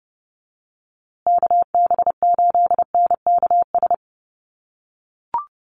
Morse code, heard as a keyed tone, about 30 wpm, 700 Hz. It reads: KB8NKS
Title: USA Call Signs Repeated 3x 30wpm